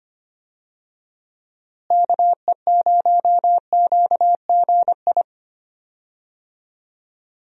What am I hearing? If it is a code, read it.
Morse code, heard as a keyed tone, about 25 wpm, 700 Hz. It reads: KE0QGI
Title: USA Call Signs - 6 Characters - Repeated 2x - 25wpm